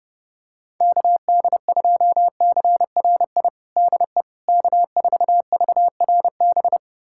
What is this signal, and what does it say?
Morse code, heard as a keyed tone, about 32 wpm, 700 Hz. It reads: KD2CRI DE K4VRB
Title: USA Callsign DE Callsign 30wpm